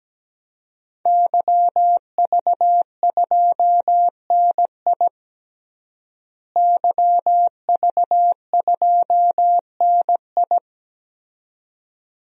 Morse code, heard as a keyed tone, about 17 wpm, 700 Hz. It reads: YV2NI YV2NI
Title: DXCC-100 Call Signs - 5 Characters - Repeated 2x - 17wpm